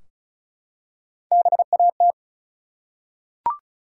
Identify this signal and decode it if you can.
Morse code, heard as a keyed tone, about 36 wpm, 700 Hz. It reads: BAT
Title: Three Letter Words - 35wpm